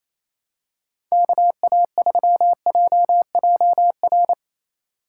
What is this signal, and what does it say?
Morse code, heard as a keyed tone, about 27 wpm, 700 Hz. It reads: KA3JJR
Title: USA Call Signs - 6 Characters - 2x Character Spacing - 28wpm